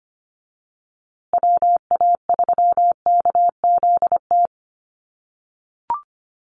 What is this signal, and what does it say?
Morse code, heard as a keyed tone, about 25 wpm, 700 Hz. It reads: WA3KZT
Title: USA Call Signs Repeated 3x 25wpm